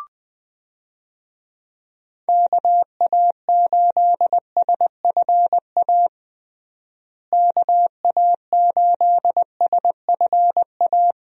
Morse code, heard as a keyed tone, about 20 wpm, 700 Hz. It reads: KA8SFA KA8SFA
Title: USA Call Signs - 6 Characters - Repeated 2x - 20wpm